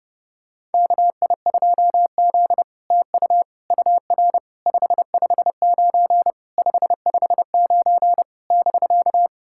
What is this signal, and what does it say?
Morse code, heard as a keyed tone, about 30 wpm, 700 Hz. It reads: KI2Z TU UR 559 559 <BK>
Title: SOTA QSO Practice 30wpm